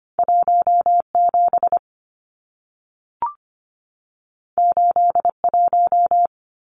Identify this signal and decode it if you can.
Morse code, heard as a keyed tone, about 25 wpm, 700 Hz. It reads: 17 81
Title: Numbers - 2-Digit 25wpm